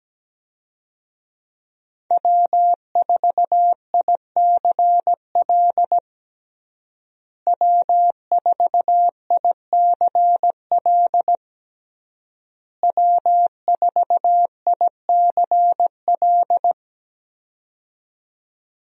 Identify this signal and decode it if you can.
Morse code, heard as a keyed tone, about 17 wpm, 700 Hz. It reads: W4ICL W4ICL W4ICL
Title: USA Call Signs - 5 Characters - Repeated 3x - 17wpm